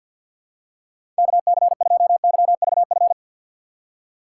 Morse code, heard as a keyed tone, about 53 wpm, 700 Hz. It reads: KC2YVF
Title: USA Call Signs - 6 Characters - 3x Character Spacing - 50wpm